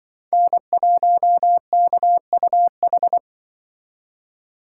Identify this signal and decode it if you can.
Morse code, heard as a keyed tone, about 24 wpm, 700 Hz. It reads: N1KUH
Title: Speed Racing - USA 1x3 Call Signs 20wpm